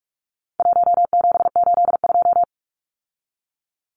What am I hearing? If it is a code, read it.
Morse code, heard as a keyed tone, about 44 wpm, 700 Hz. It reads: 1782
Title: Numbers - 4-Digit 45wpm